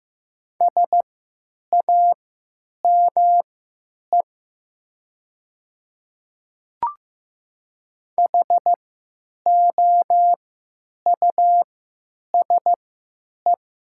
Morse code, heard as a keyed tone, about 15 wpm, spaced as for 9 wpm, 700 Hz. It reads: SAME HOUSE
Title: 3x Character Spacing - Top 200 Words - 15wpm